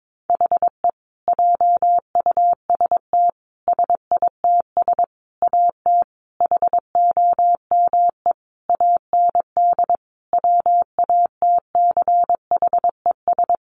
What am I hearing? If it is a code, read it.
Morse code, heard as a keyed tone, about 22 wpm, 700 Hz. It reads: HE JUST SITS AT HOME AND WATCHES
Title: Additional Easy Phrases 22wpm